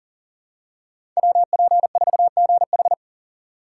Morse code, heard as a keyed tone, about 40 wpm, 700 Hz. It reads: WP4GH
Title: USA Call Signs - 5 Characters - 6x Character Spacing - 40wpm